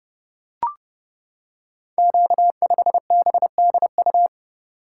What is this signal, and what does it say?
Morse code, heard as a keyed tone, about 28 wpm, 700 Hz. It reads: Q5BDU
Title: Code Groups - 5 Letter:Numbers - 30wpm